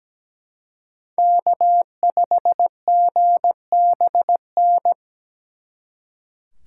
Morse code, heard as a keyed tone, about 20 wpm, 700 Hz. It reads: K5GBN
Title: Speed Racing - USA 1x3 Call Signs 17wpm